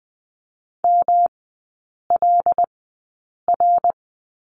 Morse code, heard as a keyed tone, about 20 wpm, spaced as for 8 wpm, 700 Hz. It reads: MLR
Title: 159 Lesson 26e - Letter g - Three Characters 20wpm